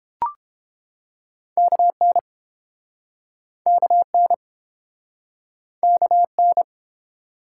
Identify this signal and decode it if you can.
Morse code, heard as a keyed tone, about 30 wpm, 700 Hz. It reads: KN KN KN